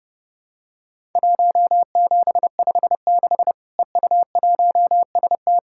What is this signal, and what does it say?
Morse code, heard as a keyed tone, about 32 wpm, 700 Hz. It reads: 1756 EU1ST
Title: Worked All Europe DX Contest - QTC Exchange - 30wpm